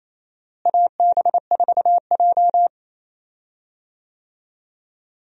Morse code, heard as a keyed tone, about 28 wpm, 700 Hz. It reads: AB4J
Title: USA Call Signs - 4 Characters - Repeated 2x - 28wpm